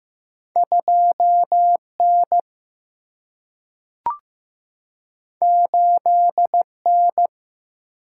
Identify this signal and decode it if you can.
Morse code, heard as a keyed tone, about 15 wpm, 700 Hz. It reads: 2 N 8 N
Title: Single-Letter Single-Number 15wpm